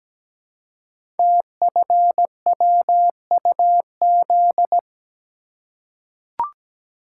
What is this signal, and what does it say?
Morse code, heard as a keyed tone, about 17 wpm, 700 Hz. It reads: TFWUZ
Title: Code Groups - 5 Letter:Numbers - 17wpm